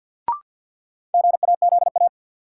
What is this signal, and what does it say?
Morse code, heard as a keyed tone, about 50 wpm, 700 Hz. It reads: GAZA
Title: Sets of 5 Words - New Word Drill - 50wpm